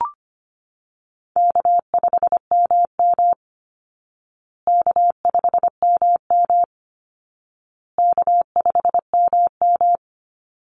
Morse code, heard as a keyed tone, about 25 wpm, 700 Hz. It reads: K5MM K5MM K5MM
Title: Call Signs Repeated 2x - 25wpm